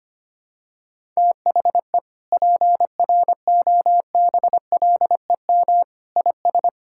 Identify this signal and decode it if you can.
Morse code, heard as a keyed tone, about 25 wpm, 700 Hz. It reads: THE PROBLEM IS
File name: practice / 25wpm / Sentences with 3-Word Phrases - 25wpm